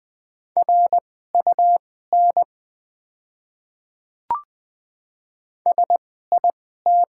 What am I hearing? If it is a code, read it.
Morse code, heard as a keyed tone, about 20 wpm, spaced as for 14 wpm, 690 Hz. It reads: RUN SIT
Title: 2x Character Spacing - Top 500 Words - 20wpm